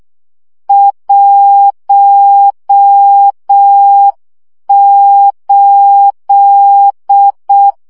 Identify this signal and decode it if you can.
Morse code, heard as a keyed tone, about 6 wpm, 800 Hz. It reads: 18
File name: morze vizsga 2011-2 5